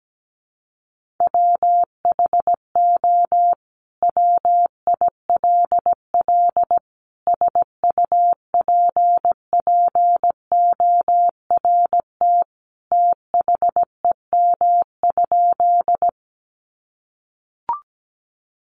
Morse code, heard as a keyed tone, about 17 wpm, 700 Hz. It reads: WHO WILL SUPPORT THEM?
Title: Sentences from Top 800 Words 17wpm